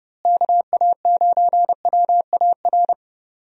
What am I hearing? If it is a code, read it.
Morse code, heard as a keyed tone, about 30 wpm, 700 Hz. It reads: KA9WAR
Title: USA Call Signs - 6 Characters - Repeated 2x - 30wpm